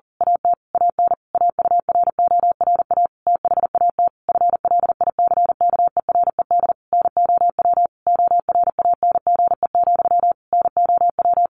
Morse code, heard as a keyed tone, about 40 wpm, 700 Hz. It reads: AT AN AURORA THAT FLICKERED NOW ORANGE, NOW
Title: A Sound Of Thunder 40wpm